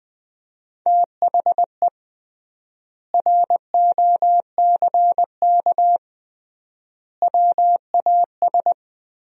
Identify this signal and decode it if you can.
Morse code, heard as a keyed tone, about 20 wpm, 700 Hz. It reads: THE ROCK WAS
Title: 3x Word-Spacing - Sentences from Top 400 Words - 20wpm